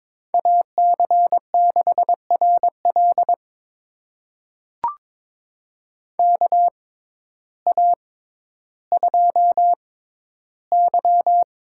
Morse code, heard as a keyed tone, about 23 wpm, spaced as for 12 wpm, 700 Hz. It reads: AC6RL KA2Y
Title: USA Call Signs - 6x Character Spacing - 22wpm